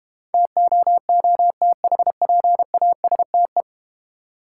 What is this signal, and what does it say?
Morse code, heard as a keyed tone, about 31 wpm, 700 Hz. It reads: TOOTHPASTE
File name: Easy Words - Speed-Racing with Hints - 25wpm